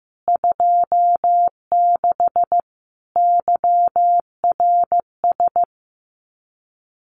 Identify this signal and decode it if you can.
Morse code, heard as a keyed tone, about 15 wpm, 700 Hz. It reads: 26 YRS
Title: Common QSO Phrases Encore 15wpm